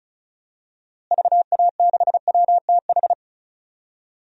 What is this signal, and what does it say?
Morse code, heard as a keyed tone, about 36 wpm, 700 Hz. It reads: VA6WTH
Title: DXCC-100 Call Signs - 6 Characters - Repeated 2x - 35wpm